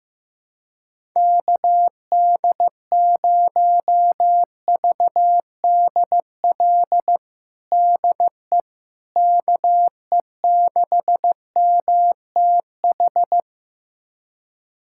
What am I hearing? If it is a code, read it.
Morse code, heard as a keyed tone, about 15 wpm, 700 Hz. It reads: KD0VDL DE KE6MTH
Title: USA Callsign DE Callsign - 6 Characters - 15wpm